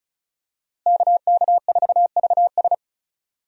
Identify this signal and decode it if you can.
Morse code, heard as a keyed tone, about 36 wpm, 700 Hz. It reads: KK4VS
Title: USA Call Signs - 5 Characters - Repeated 3x - 35wpm